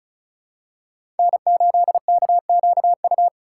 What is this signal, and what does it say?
Morse code, heard as a keyed tone, about 36 wpm, 700 Hz. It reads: N8KQU
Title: USA Call Signs - 5 Characters - 8x Character Spacing - 35wpm